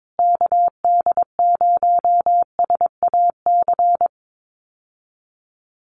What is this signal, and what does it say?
Morse code, heard as a keyed tone, about 22 wpm, 700 Hz. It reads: KD0SAC
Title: USA Call Signs Repeated 2x 22wpm